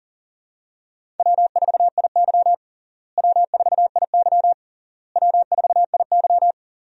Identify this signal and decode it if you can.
Morse code, heard as a keyed tone, about 40 wpm, 700 Hz. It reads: W4IY W4IY W4IY
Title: USA Call Signs - 4 Characters - Repeated 3x - 40wpm